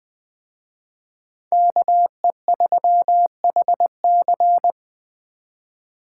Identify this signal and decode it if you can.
Morse code, heard as a keyed tone, about 20 wpm, 700 Hz. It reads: KE3HC
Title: Speed Racing - USA 2x2 Call Signs 17wpm